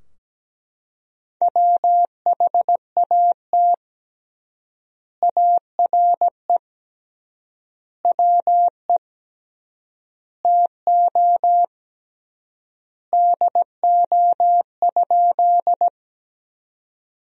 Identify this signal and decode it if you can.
Morse code, heard as a keyed tone, about 17 wpm, 700 Hz. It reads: WHAT ARE WE TO DO?
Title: 3x Word-Spacing - Sentences from Top 100 Words - 17wpm